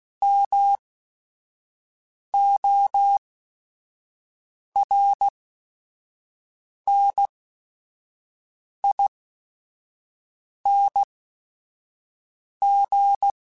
Morse code, heard as a keyed tone, about 16 wpm, spaced as for 5 wpm, 800 Hz. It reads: MORNING
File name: Weather Sets / Weather Sets-mp3s / Weather 3 17 @5